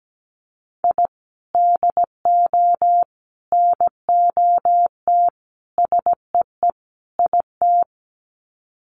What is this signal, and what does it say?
Morse code, heard as a keyed tone, about 17 wpm, 700 Hz. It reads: I DO NOT SEE IT
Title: Sets of 5 Words Encore2 17wpm